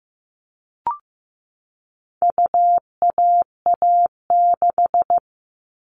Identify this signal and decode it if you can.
Morse code, heard as a keyed tone, about 15 wpm, 700 Hz. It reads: UAA6
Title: All Cut Numbers - 4-Digit 15wpm